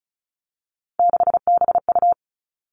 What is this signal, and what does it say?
Morse code, heard as a keyed tone, about 36 wpm, 700 Hz. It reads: 6BU
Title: Three Challenging Letter-Number Combinations 35wpm